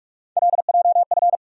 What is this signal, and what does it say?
Morse code, heard as a keyed tone, about 47 wpm, 700 Hz. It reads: LJF
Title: USA Call Sign Postfixes - 3 Letters 45wpm